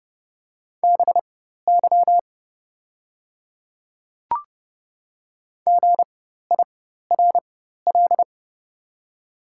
Morse code, heard as a keyed tone, about 30 wpm, spaced as for 14 wpm, 700 Hz. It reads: BY GIRL
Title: 4x Character Spacing - Top 500 Words - 30wpm